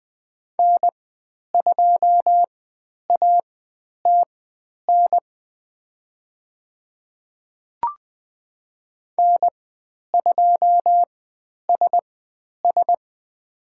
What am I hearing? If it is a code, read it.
Morse code, heard as a keyed tone, about 20 wpm, spaced as for 10 wpm, 700 Hz. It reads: N2ATN N2SS
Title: 109 Lesson 18g - Letter m - Call Signs 20wpm